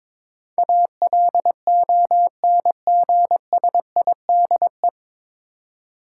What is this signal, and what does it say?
Morse code, heard as a keyed tone, about 21 wpm, 700 Hz. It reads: ALONGSIDE
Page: Morse Code Ninja practice 22wpm